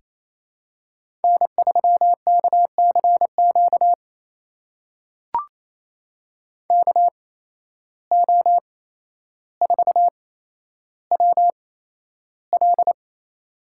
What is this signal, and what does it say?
Morse code, heard as a keyed tone, about 28 wpm, spaced as for 12 wpm, 700 Hz. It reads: N3KCQ KO4WL
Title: USA Call Signs - 5 Characters - 8x Character Spacing - 28wpm